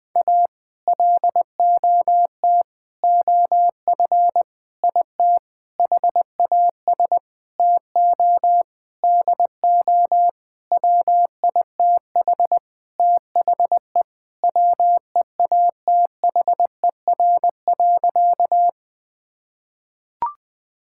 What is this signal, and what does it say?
Morse code, heard as a keyed tone, about 20 wpm, 700 Hz. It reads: A LOT OF IT HAS TO DO WITH THE WEATHER.
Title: Sentences with 9-Word Phrases - 20wpm